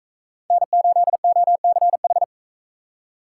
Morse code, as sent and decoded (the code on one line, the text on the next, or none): -. ---.. --- -.-. ....
N8OCH